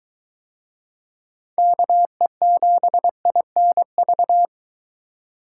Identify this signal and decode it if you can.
Morse code, heard as a keyed tone, about 23 wpm, 700 Hz. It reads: KE7INV